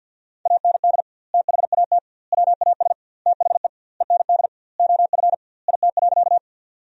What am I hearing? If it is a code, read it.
Morse code, heard as a keyed tone, about 50 wpm, 700 Hz. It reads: AND THAT WAS THE END OF IT.
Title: 2x Word-Spacing - Sentences from Top 200 Words - 50wpm